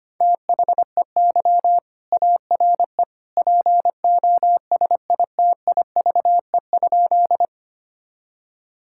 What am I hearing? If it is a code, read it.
Morse code, heard as a keyed tone, about 25 wpm, 700 Hz. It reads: THEY ARE POSITIVE?
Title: Sentences from Top 1000 Words - Review Part 5 - 25wpm